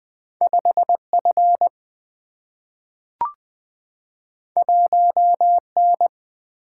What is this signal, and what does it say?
Morse code, heard as a keyed tone, about 20 wpm, 700 Hz. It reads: 5 F 1 N
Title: Single-Letter Single-Number 20wpm